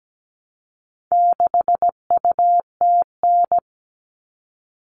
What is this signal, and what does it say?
Morse code, heard as a keyed tone, about 17 wpm, 700 Hz. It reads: 6UTN
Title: All Cut Numbers - 4-Digit 17wpm